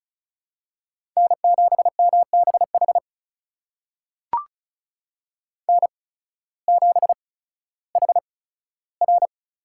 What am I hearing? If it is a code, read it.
Morse code, heard as a keyed tone, about 34 wpm, spaced as for 16 wpm, 700 Hz. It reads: N7MBH N7HR
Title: USA Call Signs - 5 Characters - 8x Character Spacing - 35wpm